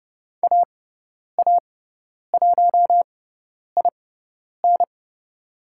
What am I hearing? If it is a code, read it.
Morse code, heard as a keyed tone, about 30 wpm, spaced as for 10 wpm, 700 Hz. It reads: AA1IN
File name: Lesson 16g - Letter c - Call Signs 30wpm